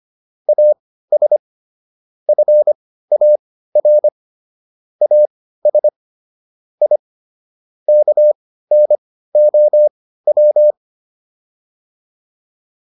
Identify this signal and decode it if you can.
Morse code, heard as a keyed tone, about 25 wpm, spaced as for 15 wpm, 600 Hz. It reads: AS FAR AS I KNOW